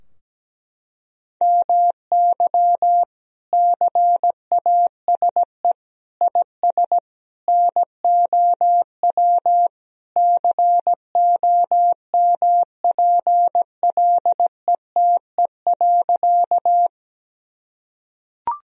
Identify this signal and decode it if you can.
Morse code, heard as a keyed tone, about 17 wpm, 700 Hz. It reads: MY CASE IS NOW COMPLETE.
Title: Sentences from Top 1000 Words - Review Part 2 - 17wpm